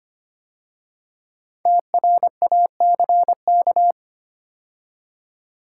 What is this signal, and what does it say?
Morse code, heard as a keyed tone, about 25 wpm, 700 Hz. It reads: TRACK
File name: 5 Letter Words - First 3 Letters Given - 25wpm